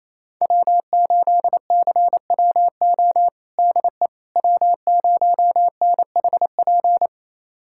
Morse code, heard as a keyed tone, about 28 wpm, 700 Hz. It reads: W8CWO DE W0NHP
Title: USA Callsign DE Callsign - 5 Characters - 28wpm